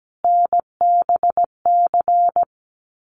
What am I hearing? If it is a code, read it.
Morse code, heard as a keyed tone, about 17 wpm, 700 Hz. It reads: NBC
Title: Sets of 5 Words Encore 17wpm